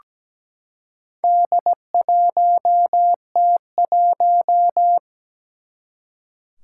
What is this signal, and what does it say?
Morse code, heard as a keyed tone, about 17 wpm, 700 Hz. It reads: D1T1
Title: Four Challenging Letter-Number Combinations 17wpm